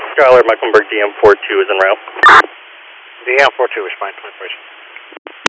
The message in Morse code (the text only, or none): none